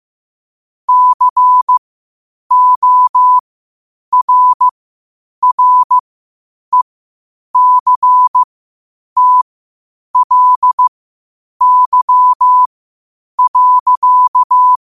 Morse code, heard as a keyed tone, about 15 wpm, spaced as for 8 wpm, 1000 Hz. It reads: CORRECTLY.
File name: qotdfiles / 8wpm